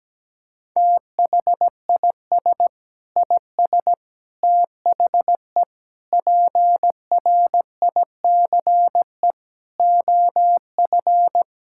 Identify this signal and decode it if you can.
Morse code, heard as a keyed tone, about 17 wpm, 700 Hz. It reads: THIS IS THE PRICE OF
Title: Sentences from Top 1000 Words - Review Part 5 - 17wpm